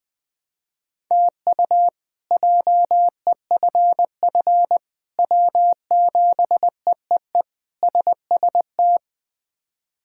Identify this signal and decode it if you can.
Morse code, heard as a keyed tone, about 20 wpm, 700 Hz. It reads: TU JEFF W7EEE SST
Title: K1USN SST QSO Practice - 20wpm